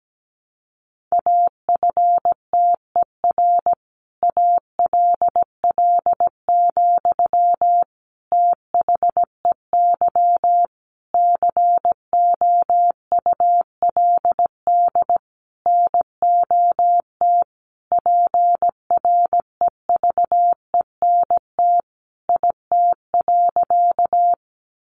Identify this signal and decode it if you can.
Morse code, heard as a keyed tone, about 17 wpm, 700 Hz. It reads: AFTER ALL, THEY COULD NOT PREVENT IT.
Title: Sentences from Top 1100 Words 17wpm